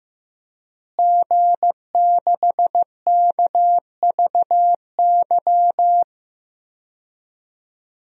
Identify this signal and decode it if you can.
Morse code, heard as a keyed tone, about 15 wpm, 700 Hz. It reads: G6KVY
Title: DXCC-100 Call Signs - 5 Characters - Repeated 2x - 15wpm